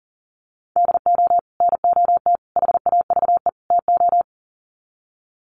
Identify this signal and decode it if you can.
Morse code, heard as a keyed tone, about 40 wpm, 700 Hz. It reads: DO NOT HAVE TO
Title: Sets of 4 Words 40wpm